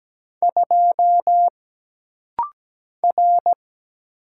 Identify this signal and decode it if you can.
Morse code, heard as a keyed tone, about 17 wpm, 700 Hz. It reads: 2 R
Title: Single Letter-Number Alt - Rapid-Fire 17wpm